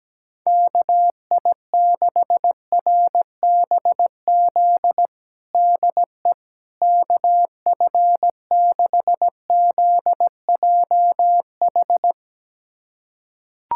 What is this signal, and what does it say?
Morse code, heard as a keyed tone, about 17 wpm, 700 Hz. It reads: KI6RBZ DE KF6ZJH
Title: USA Callsign DE Callsign - 6 Characters - 17wpm